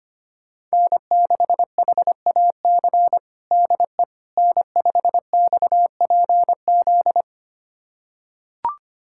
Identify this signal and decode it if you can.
Morse code, heard as a keyed tone, about 25 wpm, 700 Hz. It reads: N6HAC DE N5XPZ
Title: USA Callsign DE Callsign - 5 Characters - 25wpm